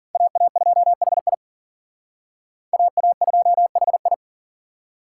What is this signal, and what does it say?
Morse code, heard as a keyed tone, about 47 wpm, 700 Hz. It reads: AA2HI AA2HI
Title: Speed Racing - USA 2x2 Call Signs 40wpm